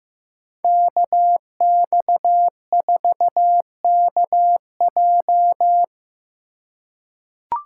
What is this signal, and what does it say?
Morse code, heard as a keyed tone, about 16 wpm, 700 Hz. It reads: KX4KJ
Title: USA Call Signs - 5 Characters - 4x Character Spacing - 15wpm